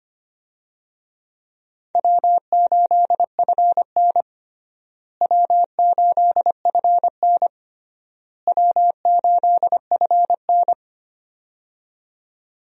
Morse code, heard as a keyed tone, about 25 wpm, 700 Hz. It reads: W8FN W8FN W8FN
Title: USA Call Signs - 4 Characters - Repeated 3x - 25wpm